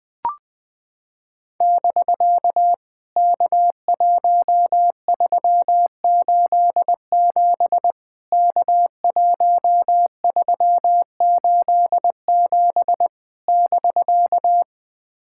Morse code, heard as a keyed tone, about 20 wpm, 700 Hz. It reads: <BK> K1387 K1387 <BK>
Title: POTA QSO Practice - 20wpm